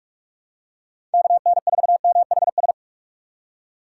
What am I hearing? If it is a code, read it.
Morse code, heard as a keyed tone, about 41 wpm, 700 Hz. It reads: KN4MHS
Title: USA Call Signs - 6 Characters - 2x Character Spacing - 45wpm